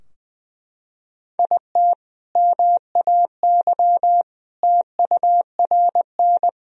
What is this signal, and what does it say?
Morse code, heard as a keyed tone, about 21 wpm, 700 Hz. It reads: IT MAY TURN
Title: Sentences from Top 1000 Words - Review Part 2 - 20wpm